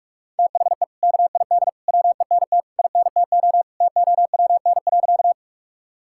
Morse code, heard as a keyed tone, about 42 wpm, 700 Hz. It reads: THE KID WENT INTO TOWN.
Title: Sentences from Top 1600 Words 45wpm